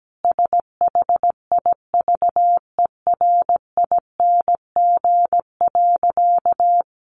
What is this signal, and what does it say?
Morse code, heard as a keyed tone, about 17 wpm, 700 Hz. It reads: SHIVERING.
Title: A Sound Of Thunder 17wpm